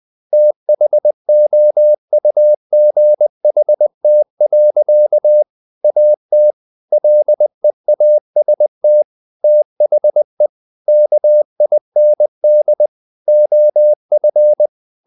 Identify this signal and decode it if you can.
Morse code, heard as a keyed tone, about 20 wpm, 600 Hz. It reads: THOUGHT. AT LEAST THE KIND OF
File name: the old man and the sea part 2 20wpm